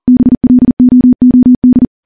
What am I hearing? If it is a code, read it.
Morse code, heard as a keyed tone, about 40 wpm, 250 Hz. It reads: BLOOD
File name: morse-training-40WPM